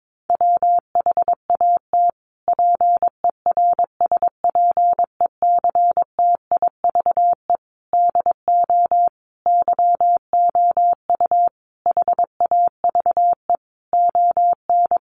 Morse code, heard as a keyed tone, about 22 wpm, 700 Hz. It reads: WHAT PERSPECTIVE DO YOU HAVE ON
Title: Sentences from Top 1200 Words 22wpm